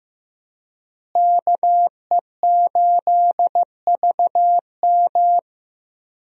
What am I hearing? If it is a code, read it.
Morse code, heard as a keyed tone, about 15 wpm, 700 Hz. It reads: KE8VM
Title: USA Call Signs - 5 Characters - Repeated 3x - 15wpm